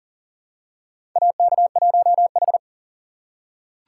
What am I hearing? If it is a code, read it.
Morse code, heard as a keyed tone, about 40 wpm, 700 Hz. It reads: AK1H
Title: USA Call Signs - 4 Characters - 4x Character Spacing - 40wpm